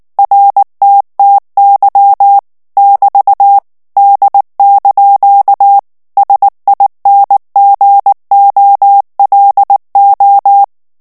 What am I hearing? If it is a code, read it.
Morse code, heard as a keyed tone, about 19 wpm, 800 Hz. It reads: RTTY <BT> D) SINGOLO
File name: morse mp3 / ITALIEN texte clair 2 vitesse 18 mots minute